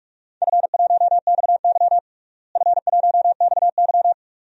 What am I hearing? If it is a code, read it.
Morse code, heard as a keyed tone, about 47 wpm, 700 Hz. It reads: F1XY F1XY
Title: DXCC-100 Call Signs - 4 Characters - Repeated 2x - 45wpm